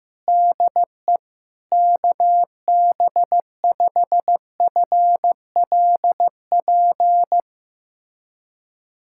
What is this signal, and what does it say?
Morse code, heard as a keyed tone, about 15 wpm, 700 Hz. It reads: DE KB5FLP
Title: USA Callsign DE Callsign - 6 Characters - 15wpm